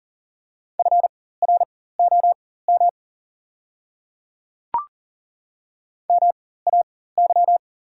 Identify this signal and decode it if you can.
Morse code, heard as a keyed tone, about 40 wpm, spaced as for 19 wpm, 690 Hz. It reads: FROM MAY
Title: 4x Character Spacing - Top 200 Words - 40wpm